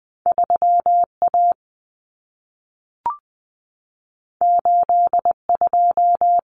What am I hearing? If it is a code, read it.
Morse code, heard as a keyed tone, about 20 wpm, 700 Hz. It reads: 3A 82
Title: Common Cut Numbers - 2-Digit 20wpm